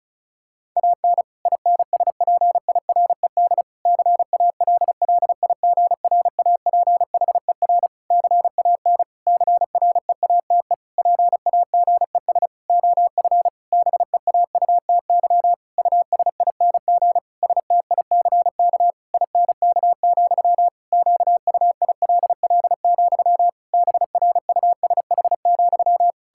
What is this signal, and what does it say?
Morse code, heard as a keyed tone, about 34 wpm, 700 Hz. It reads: AN INSPIRED CALLIGRAPHER CAN CREATE PAGES OF BEAUTY USING STICK INK, QUILL, BRUSH,
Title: Pangram Challenge v2 - 35wpm